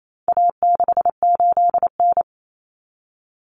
Morse code, heard as a keyed tone, about 28 wpm, 700 Hz. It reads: A68N
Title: Common Cut Numbers - 4-Digit 28wpm